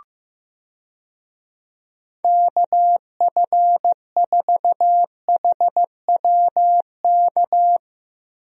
Morse code, heard as a keyed tone, about 15 wpm, 700 Hz. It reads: KF4HWK
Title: USA Call Signs - 6 Characters - Repeated 2x - 15wpm